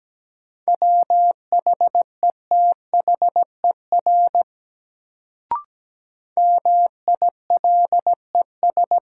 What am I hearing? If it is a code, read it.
Morse code, heard as a keyed tone, about 17 wpm, 700 Hz. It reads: WHETHER MILES
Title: Sentences from Top 1000 Words - Review Part 3 - New Word Drill - 17wpm